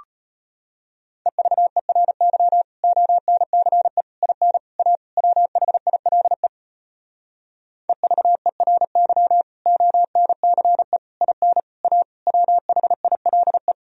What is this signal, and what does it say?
Morse code, heard as a keyed tone, about 33 wpm, 700 Hz. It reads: EVERY ONCE IN A WHILE EVERY ONCE IN A WHILE